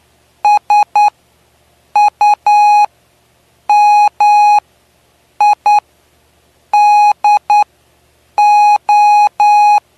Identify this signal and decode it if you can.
Morse code, heard as a keyed tone, about 9 wpm, spaced as for 6 wpm, 830 Hz. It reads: SUMIDO
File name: cw / py2lcd / 09-PY2LCD-Li��o 9